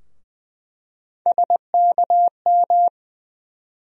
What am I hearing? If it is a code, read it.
Morse code, heard as a keyed tone, about 21 wpm, 700 Hz. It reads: SKM
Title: USA Call Sign Postfixes - 3 Letters 20wpm